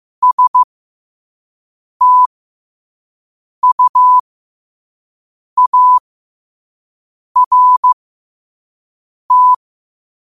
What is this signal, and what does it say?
Morse code, heard as a keyed tone, about 15 wpm, spaced as for 5 wpm, 1000 Hz. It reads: STUART
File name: 5wpm